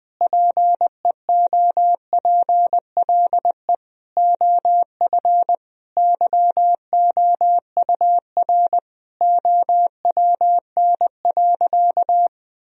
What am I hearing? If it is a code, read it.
Morse code, heard as a keyed tone, about 20 wpm, 700 Hz. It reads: PEOPLE OF YOUR OWN.
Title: Sentences from Top 1000 Words - Review Part 2 - 20wpm